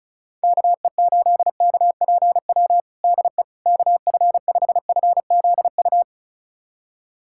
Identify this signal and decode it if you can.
Morse code, heard as a keyed tone, about 36 wpm, 700 Hz. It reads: KE8KPW DE KF5FZU
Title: USA Callsign DE Callsign 35wpm